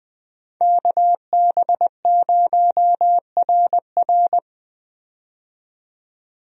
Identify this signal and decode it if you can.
Morse code, heard as a keyed tone, about 21 wpm, 700 Hz. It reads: KB0RR
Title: USA Call Signs - 5 Characters - 20wpm